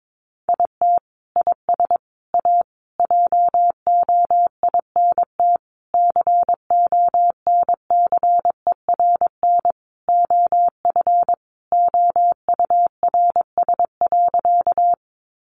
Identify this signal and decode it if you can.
Morse code, heard as a keyed tone, about 22 wpm, 700 Hz. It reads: IT IS A JOINT CONCERN OF OURS.
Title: Sentences from Top 1300 Words 22wpm